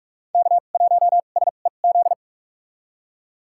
Morse code, heard as a keyed tone, about 46 wpm, spaced as for 33 wpm, 700 Hz. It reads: K1SEZ
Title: USA Call Signs - 5 Characters - 2x Character Spacing - 45wpm